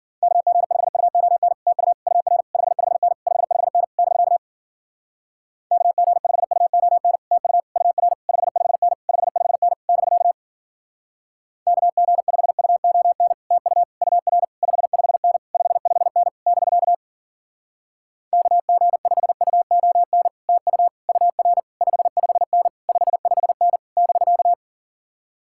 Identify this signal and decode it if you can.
Morse code, heard as a keyed tone, about 51 wpm, 700 Hz. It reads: KG5UON TU UR 55N 55N <BK> KG5UON TU UR 55N 55N <BK> KG5UON TU UR 55N 55N <BK> KG5UON TU UR 55N 55N <BK>